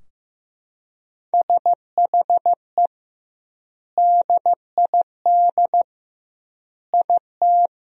Morse code, heard as a keyed tone, about 15 wpm, 700 Hz. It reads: SHE DID IT
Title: 2x Word-Spacing - Sentences from Top 400 Words - 15wpm